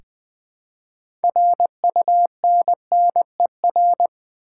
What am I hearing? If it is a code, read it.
Morse code, heard as a keyed tone, about 20 wpm, 700 Hz. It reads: RUNNER
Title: Build A Word - 3 Letters At A Time - 6-Letter Words - 20wpm